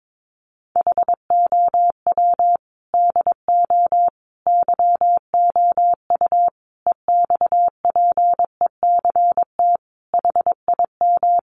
Morse code, heard as a keyed tone, about 22 wpm, 700 Hz. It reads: HOW DO YOU EXPECT HIM
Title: Sentences from Top 1100 Words 22wpm